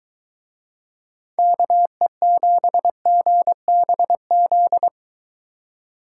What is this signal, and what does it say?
Morse code, heard as a keyed tone, about 21 wpm, 700 Hz. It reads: KE7GBZ